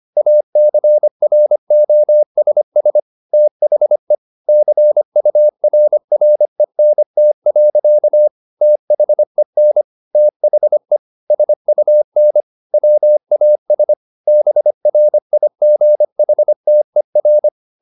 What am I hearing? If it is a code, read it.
Morse code, heard as a keyed tone, about 25 wpm, 600 Hz. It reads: ACROSS THE CURRENT. THEN THE SUN WAS BRIGHTER
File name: the old man and the sea part 1 25wpm